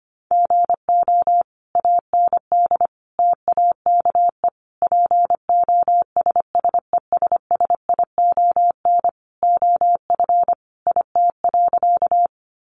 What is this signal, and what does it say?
Morse code, heard as a keyed tone, about 25 wpm, 700 Hz. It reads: GO AND TAKE POSSESSION OF IT.
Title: Sentences from Top 1400 Words 25wpm